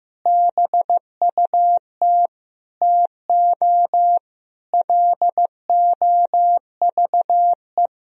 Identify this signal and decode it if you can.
Morse code, heard as a keyed tone, about 15 wpm, 700 Hz. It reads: BUT TO LOVE
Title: Sentences from Top 1000 Words - Review Part 4 - 15wpm